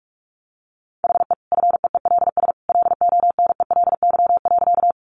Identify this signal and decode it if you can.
Morse code, heard as a keyed tone, about 44 wpm, 700 Hz. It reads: HE FEELS LONELY.
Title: Easy Phrases 45wpm